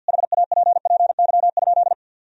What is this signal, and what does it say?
Morse code, heard as a keyed tone, about 50 wpm, 700 Hz. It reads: HAPPY?
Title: Sentences from Top 1000 Words - Review Part 2 - 50wpm